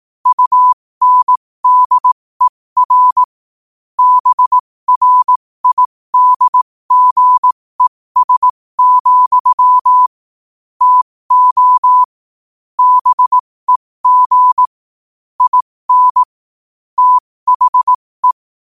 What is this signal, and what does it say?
Morse code, heard as a keyed tone, about 18 wpm, spaced as for 15 wpm, 1000 Hz. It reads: UNDER BRIDGES, TO BEG IN THE
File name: qotdfiles / 15wpm